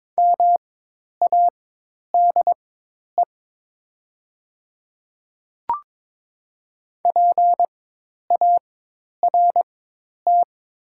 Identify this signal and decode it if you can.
Morse code, heard as a keyed tone, about 22 wpm, spaced as for 10 wpm, 700 Hz. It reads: MADE PART
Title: 4x Character Spacing - Top 200 Words - 22wpm